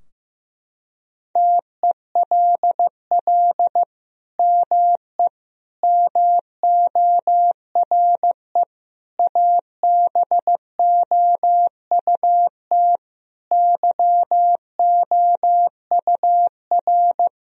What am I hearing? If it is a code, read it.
Morse code, heard as a keyed tone, about 15 wpm, 700 Hz. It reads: TELL ME MORE ABOUT YOUR
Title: Sentences from Top 1000 Words - Review Part 4 - 15wpm